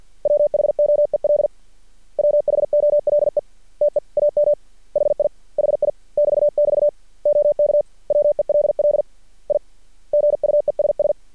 Morse code, heard as a keyed tone, about 50 wpm, 590 Hz. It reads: WHOEL WHOLE TEAM HI HI <BT><BT> OK WELL I GUESS
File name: archived / CRSnet